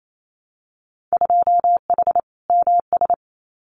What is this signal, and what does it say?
Morse code, heard as a keyed tone, about 28 wpm, 700 Hz. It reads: 2H MS
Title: Winter Field Day Exchanges 28wpm